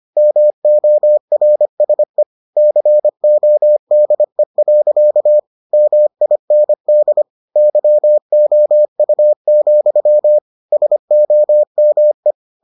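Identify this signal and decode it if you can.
Morse code, heard as a keyed tone, about 25 wpm, 600 Hz. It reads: MORSE CODE. MIND YOU, SOME
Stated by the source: Morse edition of the Foundations of Amateur Radio episodes.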